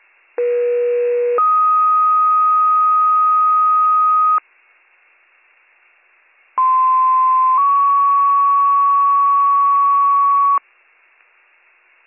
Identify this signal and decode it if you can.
Two-tone page, heard as a one-second tone, then a three-second tone, about 480 then 1250 Hz, about 1040 then 1120 Hz. It is heard from 0.4 to 10.6 s.